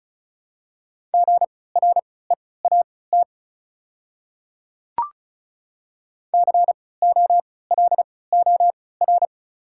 Morse code, heard as a keyed tone, about 36 wpm, spaced as for 20 wpm, 690 Hz. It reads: GREAT COLOR
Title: 3x Character Spacing - Top 500 Words - 35wpm